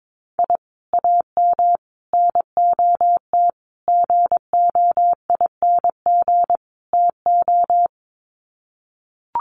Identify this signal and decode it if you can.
Morse code, heard as a keyed tone, about 22 wpm, 700 Hz. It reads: I AM NOT GOING TO
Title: Sets of 5 Words Encore 22wpm